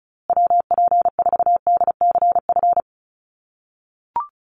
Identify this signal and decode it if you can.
Morse code, heard as a keyed tone, about 34 wpm, 700 Hz. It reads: WP4DCF
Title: USA Call Signs Repeated 2x 35wpm